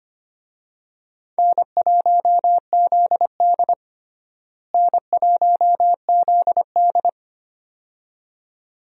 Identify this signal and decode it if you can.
Morse code, heard as a keyed tone, about 25 wpm, 700 Hz. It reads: N1ZD N1ZD
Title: USA Call Signs - 4 Characters - Repeated 2x - 25wpm